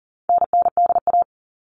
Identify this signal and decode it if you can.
Morse code, heard as a keyed tone, about 40 wpm, 700 Hz. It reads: NNDA
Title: All Cut Numbers - 4-Digit 40wpm